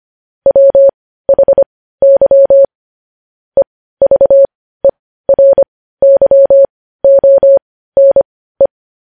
Morse code, heard as a keyed tone, about 25 wpm, spaced as for 15 wpm, 550 Hz. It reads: WHY EVERYONE
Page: news headlines in morse code at 15 wpm